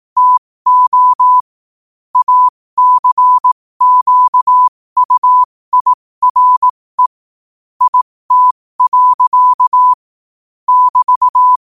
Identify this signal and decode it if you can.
Morse code, heard as a keyed tone, about 18 wpm, spaced as for 15 wpm, 1000 Hz. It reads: TO ACQUIRE IT. <BT>
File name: qotdfiles / 15wpm